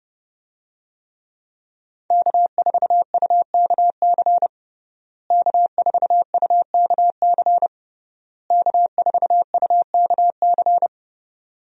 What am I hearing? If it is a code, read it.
Morse code, heard as a keyed tone, about 28 wpm, 700 Hz. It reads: K4UKC K4UKC K4UKC
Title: USA Call Signs - 5 Characters - Repeated 3x - 30wpm